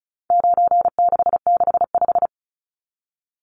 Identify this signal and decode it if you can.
Morse code, heard as a keyed tone, about 34 wpm, 700 Hz. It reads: 9665
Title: Numbers - 4-Digit 35wpm